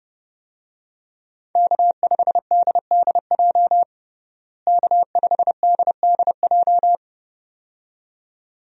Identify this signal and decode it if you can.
Morse code, heard as a keyed tone, about 30 wpm, 700 Hz. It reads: K5DDJ K5DDJ
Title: USA Call Signs - 5 Characters - Repeated 2x - 30wpm